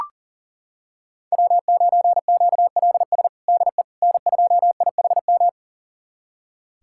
Morse code, heard as a keyed tone, about 37 wpm, 700 Hz. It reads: W9QLS DE N2IHM
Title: USA Callsign DE Callsign - 5 Characters - 40wpm